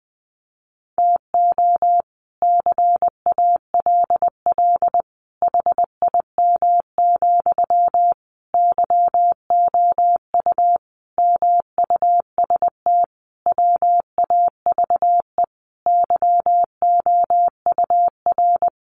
Morse code, heard as a keyed tone, about 20 wpm, 700 Hz. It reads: TO CALL HIM, YOU MUST WAVE YOUR
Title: Sentences from Top 1400 Words 20wpm